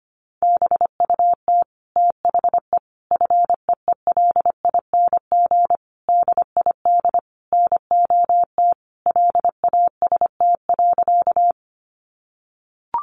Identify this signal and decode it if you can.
Morse code, heard as a keyed tone, about 25 wpm, 700 Hz. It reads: BUT THE FEELING DID NOT LAST.
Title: Sentences from Top 1200 Words 25wpm